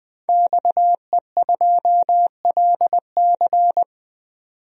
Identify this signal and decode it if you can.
Morse code, heard as a keyed tone, about 20 wpm, 700 Hz. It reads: XE2LC
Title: DXCC-100 Call Signs - 5 Characters - 20wpm